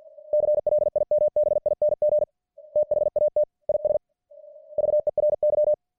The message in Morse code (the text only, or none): CLIMBING THAT IS VERY